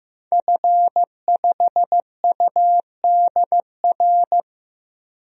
Morse code, heard as a keyed tone, about 15 wpm, 700 Hz. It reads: F5UDR
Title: DXCC-100 Call Signs - 5 Characters - 15wpm